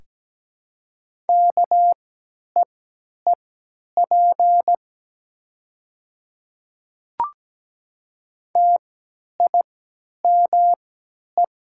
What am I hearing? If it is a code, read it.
Morse code, heard as a keyed tone, about 17 wpm, spaced as for 10 wpm, 700 Hz. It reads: KEEP TIME
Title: 3x Character Spacing - Top 300 Words - 17wpm